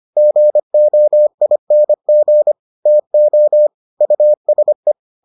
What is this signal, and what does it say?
Morse code, heard as a keyed tone, about 25 wpm, 600 Hz. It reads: GOING TO USE